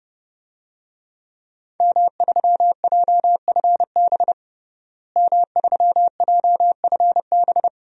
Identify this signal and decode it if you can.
Morse code, heard as a keyed tone, about 30 wpm, 700 Hz. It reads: M3JFB M3JFB
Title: DXCC-100 Call Signs - 5 Characters - Repeated 2x - 30wpm